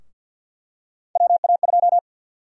Morse code, heard as a keyed tone, about 52 wpm, 700 Hz. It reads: WA2
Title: USA Call Sign Prefixes - Repeated 2x - 50wpm